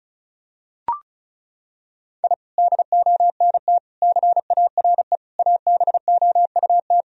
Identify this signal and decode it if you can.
Morse code, heard as a keyed tone, about 34 wpm, 700 Hz. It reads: I DONT CARE ABOUT
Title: Sentences from Top 1000 Words - Review Part 4 - 35wpm